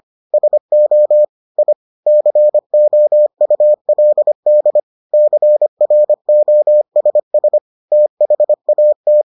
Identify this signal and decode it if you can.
Morse code, heard as a keyed tone, about 25 wpm, 600 Hz. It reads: SO I COULD CROSS THAT